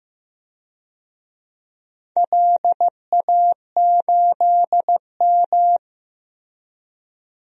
Morse code, heard as a keyed tone, about 15 wpm, 700 Hz. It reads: LA8M
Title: DXCC-100 Call Signs - 4 Characters - Repeated 2x - 15wpm